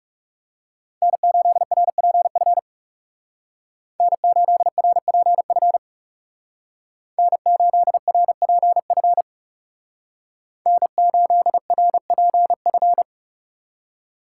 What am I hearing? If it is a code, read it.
Morse code, heard as a keyed tone, about 35 wpm, 700 Hz. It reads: N8RPF N8RPF N8RPF N8RPF
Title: Speed Racing - USA 1x3 Call Signs 30wpm